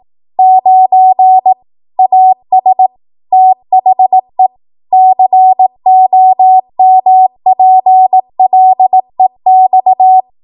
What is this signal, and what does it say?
Morse code, heard as a keyed tone, about 18 wpm, 750 Hz. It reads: 9 AS THE COMPLEX
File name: Morse / 18WPM